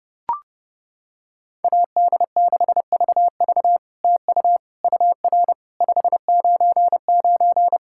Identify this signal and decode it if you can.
Morse code, heard as a keyed tone, about 30 wpm, 700 Hz. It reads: AD6VV TU UR 599
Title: SOTA QSO Practice 30wpm